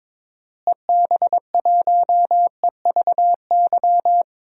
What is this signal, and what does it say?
Morse code, heard as a keyed tone, about 22 wpm, 700 Hz. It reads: EB1EVY
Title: DXCC-100 Call Signs - 6 Characters - Repeated 3x - 22wpm